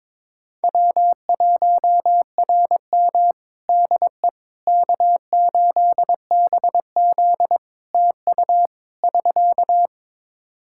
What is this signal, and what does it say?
Morse code, heard as a keyed tone, about 22 wpm, 700 Hz. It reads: W1RM DE K8BZ TU <SK>
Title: Practice QSO - Exchange 4 - Ending - 22wpm